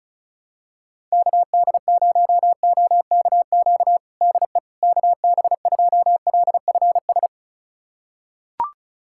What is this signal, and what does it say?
Morse code, heard as a keyed tone, about 35 wpm, 700 Hz. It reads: KD0OKQ DE KB2LFS
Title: USA Callsign DE Callsign - 6 Characters - 35wpm